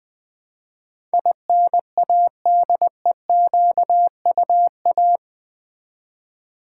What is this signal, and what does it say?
Morse code, heard as a keyed tone, about 20 wpm, 700 Hz. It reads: INADEQUA
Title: Build A Word - 4 Letters At A Time - 10-Letter Words - 20wpm